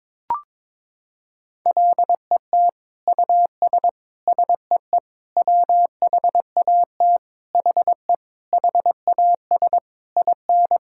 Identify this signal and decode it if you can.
Morse code, heard as a keyed tone, about 22 wpm, 700 Hz. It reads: LET US SEE WHAT HE HAS IN
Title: Sentences from Top 1000 Words - Review Part 5 - 22wpm